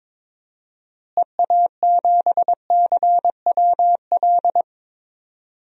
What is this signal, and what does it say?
Morse code, heard as a keyed tone, about 22 wpm, 700 Hz. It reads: EA7CWL
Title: DXCC-100 Call Signs - 6 Characters - Repeated 3x - 22wpm